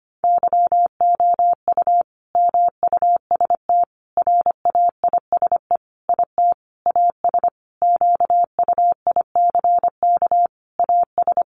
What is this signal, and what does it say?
Morse code, heard as a keyed tone, about 25 wpm, 700 Hz. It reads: YOU MUST RAISE IT AS QUICK AS
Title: Sentences from Top 1400 Words 25wpm